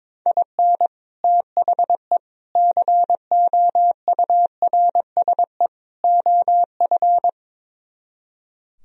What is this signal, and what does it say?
Morse code, heard as a keyed tone, about 22 wpm, 700 Hz. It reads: IN THE COURSE OF
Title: Speed-Racing - Sets of 4 Words 22wpm